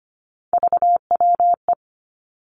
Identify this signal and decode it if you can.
Morse code, heard as a keyed tone, about 25 wpm, 700 Hz. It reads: VWE
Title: Three Challenging Letter-Number Combinations 25wpm